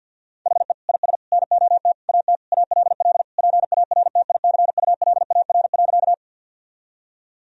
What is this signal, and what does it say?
Morse code, heard as a keyed tone, about 52 wpm, 700 Hz. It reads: HE IS NOT AT ALL PARTICULAR.
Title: Sentences from Top 1000 Words - Review Part 4 - 50wpm